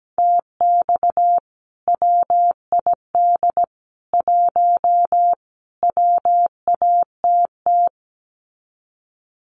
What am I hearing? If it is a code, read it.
Morse code, heard as a keyed tone, about 17 wpm, 700 Hz. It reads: TX WID 1 WATT
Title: Common QSO Phrases 17wpm